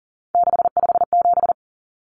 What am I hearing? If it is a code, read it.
Morse code, heard as a keyed tone, about 40 wpm, 700 Hz. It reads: 657
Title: Common Cut Numbers - 3-Digit 40wpm